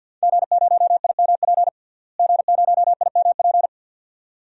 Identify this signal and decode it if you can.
Morse code, heard as a keyed tone, about 48 wpm, 700 Hz. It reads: G0IMP G0IMP
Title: DXCC-100 Call Signs - 5 Characters - Repeated 2x - 50wpm